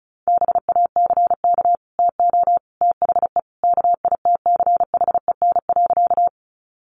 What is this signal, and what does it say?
Morse code, heard as a keyed tone, about 34 wpm, 700 Hz. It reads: BACK TO THE KITCHEN.
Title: Sentences from Top 1400 Words 35wpm